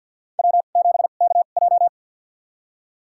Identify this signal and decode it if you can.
Morse code, heard as a keyed tone, about 52 wpm, spaced as for 36 wpm, 700 Hz. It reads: W7KJ
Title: USA Call Signs - 4 Characters - 2x Character Spacing - 50wpm